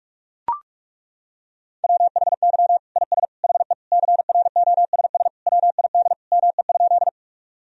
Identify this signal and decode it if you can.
Morse code, heard as a keyed tone, about 41 wpm, 700 Hz. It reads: WHY IS HE CROSS WID ME?
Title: Sentences from Top 300 Words and Abbreviations 45wpm